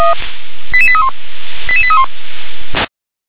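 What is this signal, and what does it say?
Five-tone selective call (ZVEI), heard as quick runs of five tones, tones of about 70 ms. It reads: 88831 88831
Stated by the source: ZVEI five-tone selective call received on an SDR.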